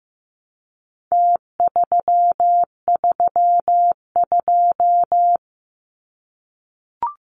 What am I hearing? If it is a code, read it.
Morse code, heard as a keyed tone, about 15 wpm, 700 Hz. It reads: T332
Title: Numbers - Contest Style - 4-Digit 15wpm